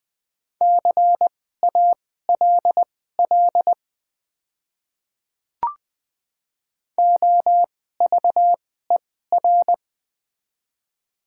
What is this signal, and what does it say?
Morse code, heard as a keyed tone, about 20 wpm, spaced as for 14 wpm, 700 Hz. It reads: CALL OVER
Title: 2x Character Spacing - Top 100 Words - 20wpm